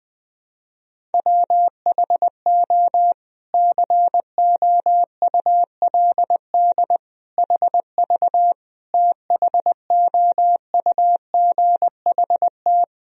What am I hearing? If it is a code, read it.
Morse code, heard as a keyed tone, about 20 wpm, 700 Hz. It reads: WHO COULD HV THOUGHT
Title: Sentences from Top 300 Words and Abbreviations 20wpm